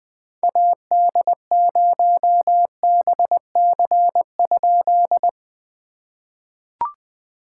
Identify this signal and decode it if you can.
Morse code, heard as a keyed tone, about 20 wpm, 700 Hz. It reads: AD0BC?
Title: POTA QSO Practice - 20wpm